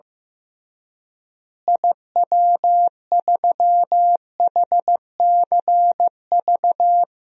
Morse code, heard as a keyed tone, about 15 wpm, 700 Hz. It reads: IW3HCV
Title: DXCC-100 Call Signs - 6 Characters - Repeated 2x - 15wpm